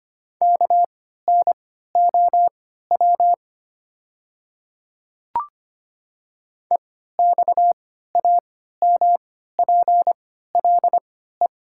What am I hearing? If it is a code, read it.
Morse code, heard as a keyed tone, about 25 wpm, spaced as for 14 wpm, 700 Hz. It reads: KNOW EXAMPLE
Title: 3x Character Spacing - Top 500 Words - 25wpm